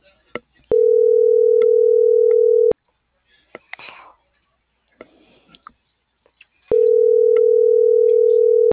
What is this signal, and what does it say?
Ringback tone (North America), 440 + 480 Hz.